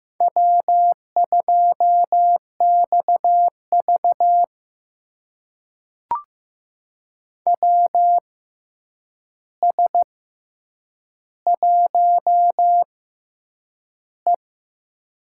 Morse code, heard as a keyed tone, about 15 wpm, spaced as for 8 wpm, 700 Hz. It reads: W2XV WS1E
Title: USA Call Signs - 4 Characters - 6x Character Spacing - 15wpm